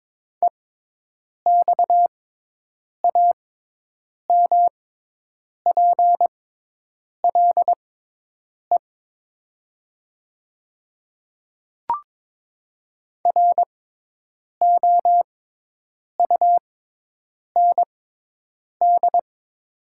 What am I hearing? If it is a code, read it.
Morse code, heard as a keyed tone, about 22 wpm, spaced as for 8 wpm, 700 Hz. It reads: EXAMPLE ROUND
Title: 6x Character Spacing - Top 500 Words - 22wpm